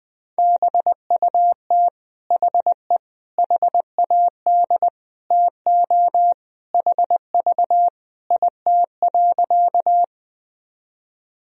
Morse code, heard as a keyed tone, about 21 wpm, 700 Hz. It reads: BUT HE HAD TO HV IT.
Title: Sentences from Top 100 Words and Abbreviations 20wpm